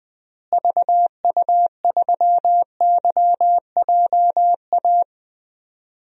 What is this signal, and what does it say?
Morse code, heard as a keyed tone, about 20 wpm, 690 Hz. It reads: VU3YJA
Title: DXCC-100 Call Signs - 6 Characters - Repeated 2x - 20wpm